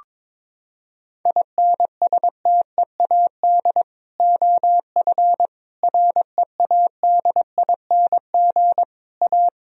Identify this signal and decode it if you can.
Morse code, heard as a keyed tone, about 22 wpm, 700 Hz. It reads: INSTEAD OF READING A
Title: Sentences from Top 1000 Words - Review Part 4 - 22wpm